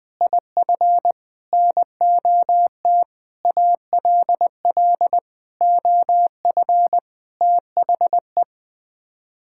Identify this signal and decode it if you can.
Morse code, heard as a keyed tone, about 20 wpm, 700 Hz. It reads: IF NOT ALL OF THE
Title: Sets of 6 Words - 20wpm